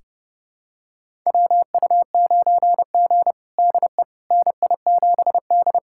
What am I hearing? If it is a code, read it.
Morse code, heard as a keyed tone, about 32 wpm, 700 Hz. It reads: WU9G DE NI7D
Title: USA Callsign DE Callsign - 4 Characters - 30wpm